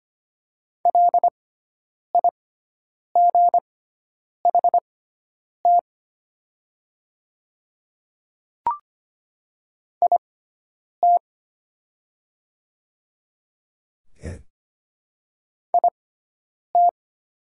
Morse code, heard as a keyed tone, about 25 wpm, spaced as for 9 wpm, 700 Hz. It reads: LIGHT IT IT
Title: 6x Character Spacing - Top 500 Words - 25wpm